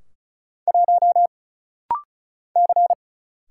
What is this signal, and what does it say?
Morse code, heard as a keyed tone, about 36 wpm, 700 Hz. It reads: 1 C
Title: Single Letter-Number Alt - Rapid-Fire 35wpm